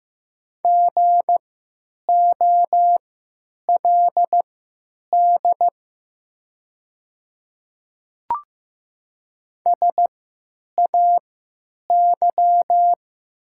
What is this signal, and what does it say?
Morse code, heard as a keyed tone, about 16 wpm, spaced as for 9 wpm, 700 Hz. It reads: GOLD SAY
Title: 3x Character Spacing - Top 500 Words - 15wpm